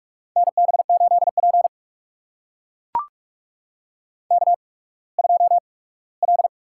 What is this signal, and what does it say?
Morse code, heard as a keyed tone, about 44 wpm, spaced as for 21 wpm, 700 Hz. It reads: NB8P K2L
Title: USA Call Signs - 4 Characters - 8x Character Spacing - 45wpm